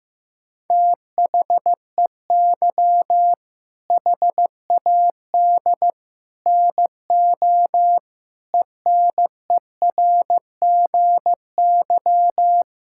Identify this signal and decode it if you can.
Morse code, heard as a keyed tone, about 15 wpm, 700 Hz. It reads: THEY HAD NO ENERGY
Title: Sentences from Top 1000 Words - Review Part 6 - 15wpm